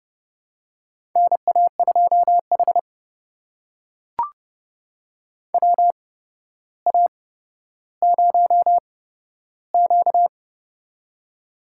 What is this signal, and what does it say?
Morse code, heard as a keyed tone, about 30 wpm, spaced as for 13 wpm, 700 Hz. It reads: NA2H WA0Q
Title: USA Call Signs - 4 Characters - 8x Character Spacing - 30wpm